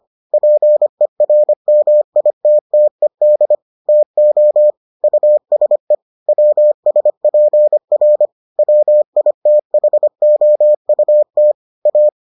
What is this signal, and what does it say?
Morse code, heard as a keyed tone, about 25 wpm, 600 Hz. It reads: PERMITTED TO USE WSPR WITHOUT A